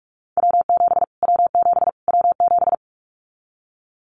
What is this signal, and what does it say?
Morse code, heard as a keyed tone, about 44 wpm, 700 Hz. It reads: W7 W7 W7
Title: Callsign Prefixes for Top 25 DXCC - Repeated 3x - 45wpm